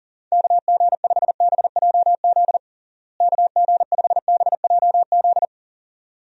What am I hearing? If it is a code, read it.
Morse code, heard as a keyed tone, about 40 wpm, 700 Hz. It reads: KG5BJZ KG5BJZ
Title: USA Call Signs - 6 Characters - Repeated 2x - 40wpm